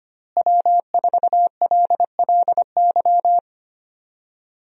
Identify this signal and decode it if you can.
Morse code, heard as a keyed tone, about 25 wpm, 700 Hz. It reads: W4LLY
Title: USA Call Signs - 5 Characters - Repeated 2x - 25wpm